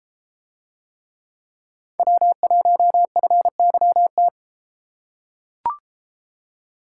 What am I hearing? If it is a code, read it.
Morse code, heard as a keyed tone, about 31 wpm, 700 Hz. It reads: W1FYT